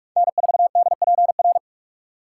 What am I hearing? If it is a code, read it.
Morse code, heard as a keyed tone, about 44 wpm, 700 Hz. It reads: N4DPR